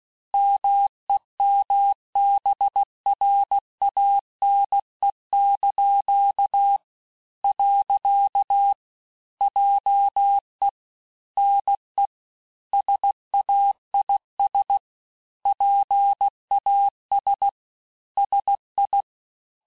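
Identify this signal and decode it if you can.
Morse code, heard as a keyed tone, about 16 wpm, 800 Hz. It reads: MEMBRANE). JE NE SAIS PAS SI
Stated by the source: Morse course text, end of QSO, sent at 15 words per minute.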